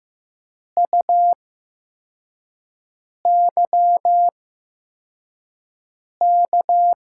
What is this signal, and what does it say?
Morse code, heard as a keyed tone, about 15 wpm, spaced as for 4 wpm, 700 Hz. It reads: UYK
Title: USA Call Signs - 6 Characters - 8x Character Spacing - 15wpm